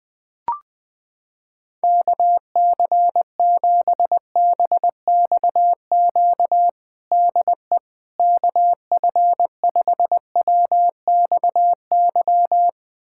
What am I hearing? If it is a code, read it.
Morse code, heard as a keyed tone, about 20 wpm, 700 Hz. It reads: KC7BXQ DE KF5WXY
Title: USA Callsign DE Callsign 20wpm